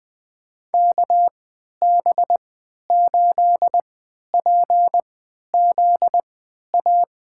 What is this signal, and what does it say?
Morse code, heard as a keyed tone, about 20 wpm, spaced as for 11 wpm, 700 Hz. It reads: KB8PZA
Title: USA Call Signs - 6 Characters - 3x Character Spacing - 20wpm